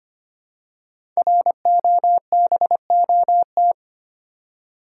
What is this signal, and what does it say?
Morse code, heard as a keyed tone, about 26 wpm, 700 Hz. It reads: ROBOT
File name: 5 Letter Words - First 3 Letters Given - 25wpm